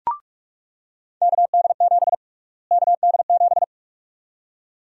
Morse code, heard as a keyed tone, about 42 wpm, 700 Hz. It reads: KD7 KD7
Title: USA Call Sign Prefixes - Repeated 2x - 45wpm